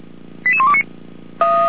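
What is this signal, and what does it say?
Five-tone selective call (ZVEI), heard as quick runs of five tones, tones of about 70 ms. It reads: 88129
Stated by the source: ZVEI five-tone selective call received on an SDR, recorded by the Feuerwehr Orlamünde fire brigade.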